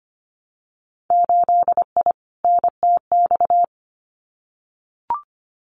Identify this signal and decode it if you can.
Morse code, heard as a keyed tone, about 25 wpm, 700 Hz. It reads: 8I NTX
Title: Winter Field Day Exchanges Encore 25wpm